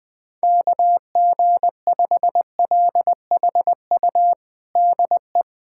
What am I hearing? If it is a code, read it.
Morse code, heard as a keyed tone, about 20 wpm, 700 Hz. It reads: KG5LHU DE
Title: USA Callsign DE Callsign - 6 Characters - 20wpm